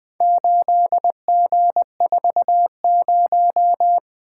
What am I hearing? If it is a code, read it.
Morse code, heard as a keyed tone, about 20 wpm, 700 Hz. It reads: 8G40
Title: Four Letter-Number Combinations - 20wpm